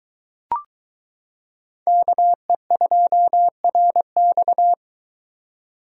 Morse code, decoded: KE2RX

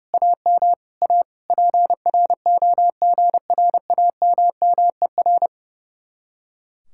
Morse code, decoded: AM A PROGRAMMER

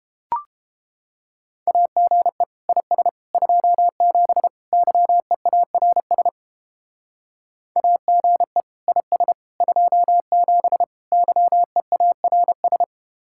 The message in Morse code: AGE IS 27 YEARS AGE IS 27 YEARS

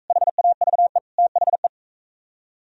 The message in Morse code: HAVE THE

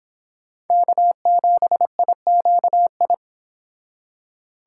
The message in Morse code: K7IQI

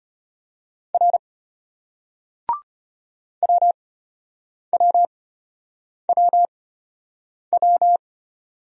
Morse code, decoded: R WWWW

